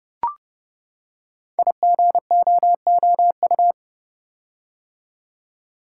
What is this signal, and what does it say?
Morse code, heard as a keyed tone, about 28 wpm, 700 Hz. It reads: IGOOU
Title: Code Groups - 5 Letters - 30wpm